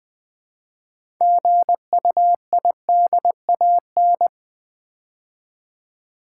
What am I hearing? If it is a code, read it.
Morse code, heard as a keyed tone, about 20 wpm, 700 Hz. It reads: GUIDAN
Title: Build A Word - 2 Letters At A Time - 8-Letter Words - 20wpm